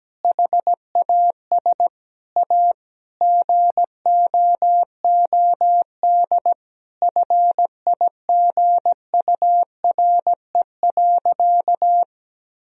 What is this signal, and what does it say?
Morse code, heard as a keyed tone, about 17 wpm, 700 Hz. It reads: HAS A GOOD FIGURE.
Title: Sentences from Top 1000 Words - Review Part 2 - 17wpm